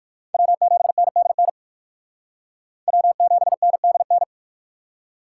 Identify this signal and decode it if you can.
Morse code, heard as a keyed tone, about 51 wpm, 700 Hz. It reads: W7NDN W7NDN
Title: Speed Racing - USA Call Signs 45wpm